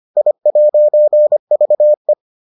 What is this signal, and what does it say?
Morse code, heard as a keyed tone, about 25 wpm, 600 Hz. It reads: I'VE